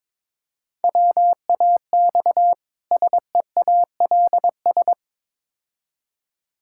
Morse code, heard as a keyed tone, about 21 wpm, 700 Hz. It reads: WAX SEALS
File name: Difficult Words with Hints - 22wpm